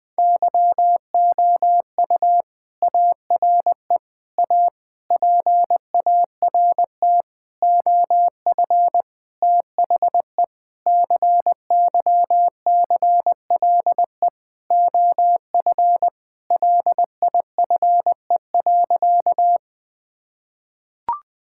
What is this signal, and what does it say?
Morse code, heard as a keyed tone, about 20 wpm, 700 Hz. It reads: YOU ARE A PART OF THE CYCLE OF LIFE.
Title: Sentences from Top 1000 Words - Review Part 6 - 20wpm